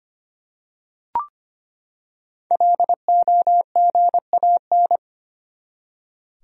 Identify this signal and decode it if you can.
Morse code, heard as a keyed tone, about 25 wpm, 700 Hz. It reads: LOGAN
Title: USA Common Names - Max 5 Letters - 25wpm